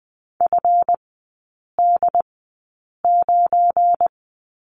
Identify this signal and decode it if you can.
Morse code, heard as a keyed tone, about 20 wpm, spaced as for 8 wpm, 700 Hz. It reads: FD9
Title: 172 Lesson 28e - Number 9 - Three Characters 20wpm